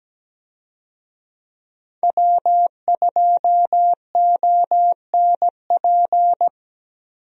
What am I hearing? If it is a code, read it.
Morse code, heard as a keyed tone, about 17 wpm, 700 Hz. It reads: W2ONP
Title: USA Call Signs - 5 Characters - Repeated 3x - 17wpm